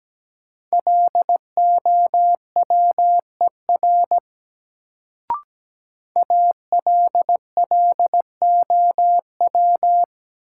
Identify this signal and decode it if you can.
Morse code, heard as a keyed tone, about 17 wpm, 700 Hz. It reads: LOWER ALLOW
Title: Sentences from Top 700 Words - New Word Drill - 17wpm